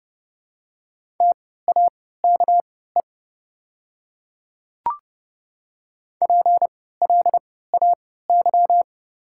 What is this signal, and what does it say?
Morse code, heard as a keyed tone, about 30 wpm, spaced as for 17 wpm, 700 Hz. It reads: TAKE PLAY
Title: 3x Character Spacing - Top 200 Words - 30wpm